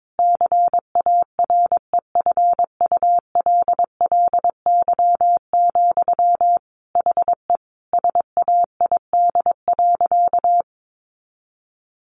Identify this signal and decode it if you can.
Morse code, heard as a keyed tone, about 22 wpm, 700 Hz. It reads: CAREFULLY, HE SAID.
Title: Sentences from Top 1300 Words 22wpm